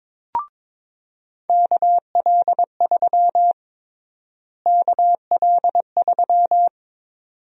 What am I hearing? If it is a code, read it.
Morse code, heard as a keyed tone, about 22 wpm, 700 Hz. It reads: KL3 KL3
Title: USA Call Sign Prefixes - Repeated 2x - 22wpm